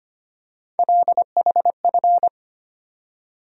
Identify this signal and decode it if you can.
Morse code, heard as a keyed tone, about 25 wpm, 700 Hz. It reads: LHF
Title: USA Call Sign Postfixes - 3 Letters 25wpm